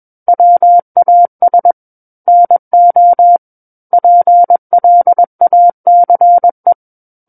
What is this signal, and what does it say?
Morse code, heard as a keyed tone, about 21 wpm, 700 Hz. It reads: WAS NO PLACE
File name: Book One / 12-Chapter 12 - 21 WPM